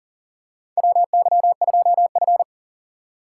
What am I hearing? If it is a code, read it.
Morse code, heard as a keyed tone, about 40 wpm, 700 Hz. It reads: WY2F
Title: USA Call Signs - 4 Characters - 6x Character Spacing - 40wpm